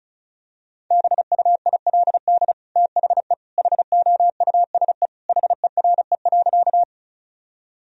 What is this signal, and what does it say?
Morse code, heard as a keyed tone, about 36 wpm, 700 Hz. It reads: BUILD THE HOUSE HERE.
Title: Sentences from Top 1000 Words - Review Part 1 - 35wpm